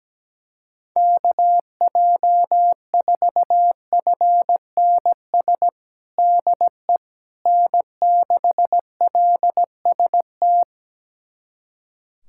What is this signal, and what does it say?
Morse code, heard as a keyed tone, about 17 wpm, 700 Hz. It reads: KJ4FNS DE N6LST
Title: USA Callsign DE Callsign 17wpm